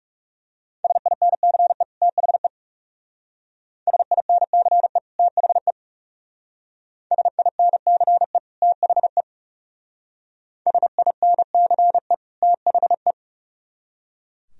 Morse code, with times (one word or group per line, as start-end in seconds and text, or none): 0.8-1.8 SINCE
2.0-2.5 THE
3.9-5.0 SINCE
5.2-5.7 THE
7.1-8.4 SINCE
8.6-9.2 THE
10.7-12.1 SINCE
12.4-13.1 THE